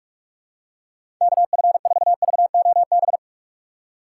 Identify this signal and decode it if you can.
Morse code, heard as a keyed tone, about 46 wpm, 700 Hz. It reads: KF4VOB
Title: USA Call Signs - 6 Characters - 2x Character Spacing - 45wpm